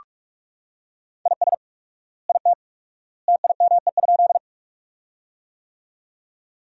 Morse code, heard as a keyed tone, about 43 wpm, 700 Hz. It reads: IS IT TIME?
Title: 4x Word-Spacing - Sentences from Top 100 Words - 45wpm